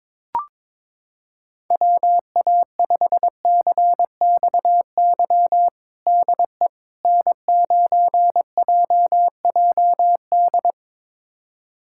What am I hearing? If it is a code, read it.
Morse code, heard as a keyed tone, about 22 wpm, 700 Hz. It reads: WA5CXY DE N9JJD
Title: USA Callsign DE Callsign 22wpm